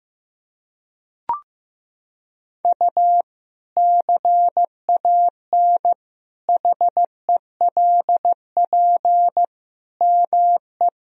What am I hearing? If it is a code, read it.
Morse code, heard as a keyed tone, about 15 wpm, 700 Hz. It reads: U CAN HELP ME
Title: Sentences from Top 200 Words and Abbreviations 15wpm